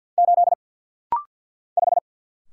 Morse code, heard as a keyed tone, about 48 wpm, 700 Hz. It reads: 8 5
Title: Single Letter-Number - Rapid-Fire 50wpm